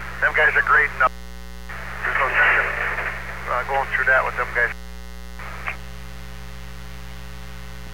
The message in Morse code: none

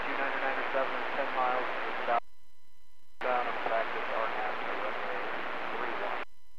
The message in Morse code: none

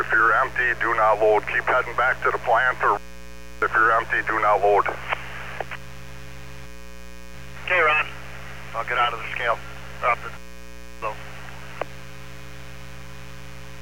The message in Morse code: none

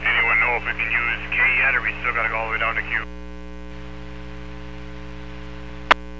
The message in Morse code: none

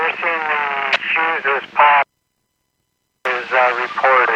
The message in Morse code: none

